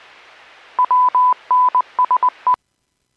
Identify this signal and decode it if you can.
Morse code, heard as a keyed tone, about 20 wpm, 1000 Hz. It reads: WNSE
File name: WPS / 00009 14-05 WPS 451.25 WPS